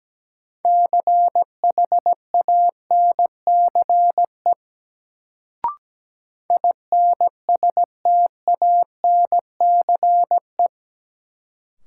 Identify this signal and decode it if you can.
Morse code, heard as a keyed tone, about 17 wpm, 700 Hz. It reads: CHANCE INSTANCE
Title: Sentences from Top 800 Words - New Word Drill - 17wpm